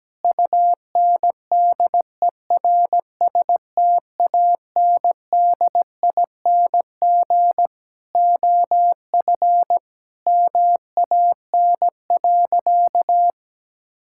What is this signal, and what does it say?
Morse code, heard as a keyed tone, about 17 wpm, 700 Hz. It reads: UNDERSTANDING OF MAN.
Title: Sentences from Top 1000 Words - Review Part 5 - 17wpm